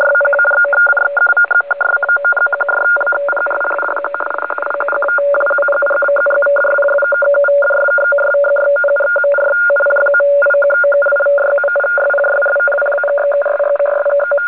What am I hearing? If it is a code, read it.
Two-tone teleprinter signal, about 850 Hz shift.
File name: 4.337.00-RTTY-15-06-2012-1